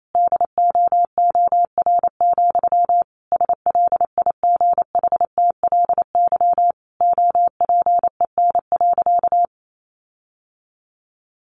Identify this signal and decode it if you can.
Morse code, heard as a keyed tone, about 28 wpm, 700 Hz. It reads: DOOR, SLIGHTLY OPEN.
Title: Sentences from Top 1300 Words 28wpm